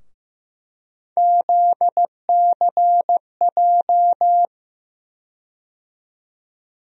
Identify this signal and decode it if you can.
Morse code, heard as a keyed tone, about 15 wpm, 700 Hz. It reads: ZCJ
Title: USA Call Sign Postfixes - 3 Letters 15wpm